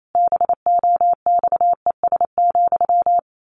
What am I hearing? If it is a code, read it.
Morse code, heard as a keyed tone, about 28 wpm, 700 Hz. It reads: BOXES,
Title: A Sound Of Thunder 28wpm